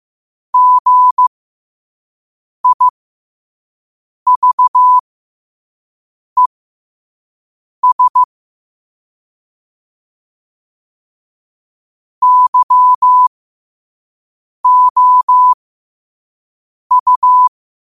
Morse code, heard as a keyed tone, about 15 wpm, spaced as for 5 wpm, 1000 Hz. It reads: GIVES YOU
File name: qotdfiles / 5wpm